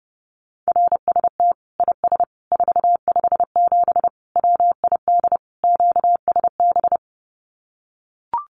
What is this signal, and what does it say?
Morse code, heard as a keyed tone, about 30 wpm, 700 Hz. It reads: RST IS 457 WID QSB
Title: Common QSO Phrases 30wpm